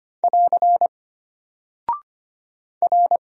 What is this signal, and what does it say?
Morse code, heard as a keyed tone, about 25 wpm, 700 Hz. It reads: <AR> R
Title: Single Character (Letter:Number:Punctuation:Prosign) Alt 25wpm